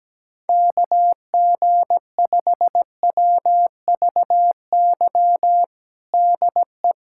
Morse code, heard as a keyed tone, about 17 wpm, 700 Hz. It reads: KG5WVY DE
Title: USA Callsign DE Callsign - 6 Characters - 17wpm